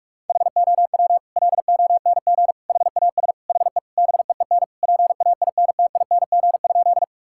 Morse code, heard as a keyed tone, about 41 wpm, 700 Hz. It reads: HOW LONG HAS HE BEEN PAINTING?
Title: Sentences from Top 1000 Words - Review Part 6 - 45wpm